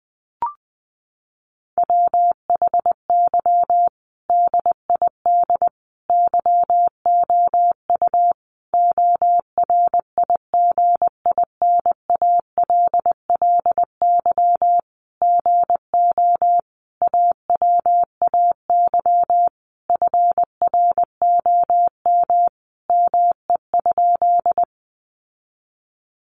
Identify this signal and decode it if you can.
Morse code, heard as a keyed tone, about 20 wpm, 700 Hz. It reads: WHY DID YOU ORIGINALLY GO AWAY FROM ME?
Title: Sentences from Top 1500 Words 20wpm